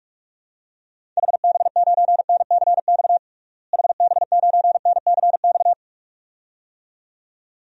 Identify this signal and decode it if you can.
Morse code, heard as a keyed tone, about 46 wpm, 700 Hz. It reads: HB9NCX HB9NCX
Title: DXCC-100 Call Signs - 6 Characters - Repeated 2x - 45wpm